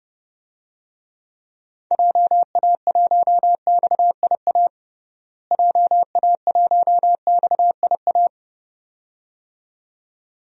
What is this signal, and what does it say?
Morse code, heard as a keyed tone, about 30 wpm, 700 Hz. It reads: JA1XIA JA1XIA
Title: DXCC-100 Call Signs - 6 Characters - Repeated 2x - 30wpm